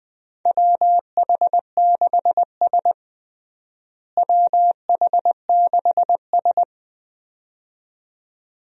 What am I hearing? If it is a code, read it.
Morse code, heard as a keyed tone, about 20 wpm, 700 Hz. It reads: WH6S WH6S
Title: USA Call Signs - 4 Characters - Repeated 2x - 20wpm